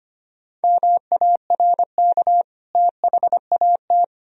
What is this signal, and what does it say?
Morse code, heard as a keyed tone, about 25 wpm, 700 Hz. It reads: MARK THAT
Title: Sentences from Top 1000 Words - Review Part 1 - 25wpm